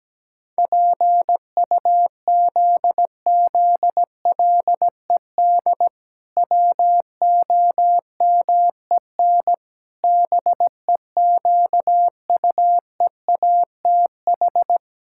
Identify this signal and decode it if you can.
Morse code, heard as a keyed tone, about 17 wpm, 700 Hz. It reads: PUZZLED WOMEN BEQUEATH
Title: Pangram Challenge v2 - 17wpm